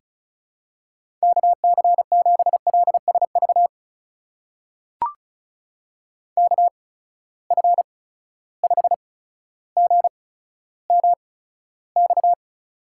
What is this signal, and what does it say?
Morse code, heard as a keyed tone, about 36 wpm, spaced as for 15 wpm, 700 Hz. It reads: KC7LSV KF5GMX
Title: USA Call Signs - 6 Characters - 8x Character Spacing - 35wpm